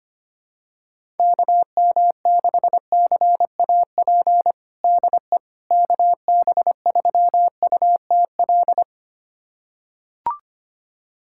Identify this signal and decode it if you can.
Morse code, heard as a keyed tone, about 25 wpm, 700 Hz. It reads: KM6CAP DE KB3UTL
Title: USA Callsign DE Callsign 25wpm